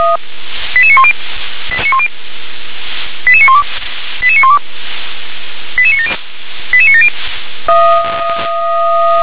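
Five-tone selective call (ZVEI), heard as quick runs of five tones, tones of about 70 ms. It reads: 88910 88912 88912 88970 88970
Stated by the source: ZVEI five-tone selective call received by SDR.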